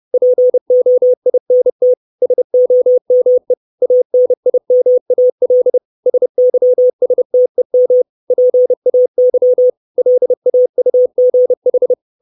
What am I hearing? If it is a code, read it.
Morse code, heard as a keyed tone, about 30 wpm, 500 Hz. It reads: POINT SOME ANIMAL SYSTEM PAY LAUGH
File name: 632 Words 30WPM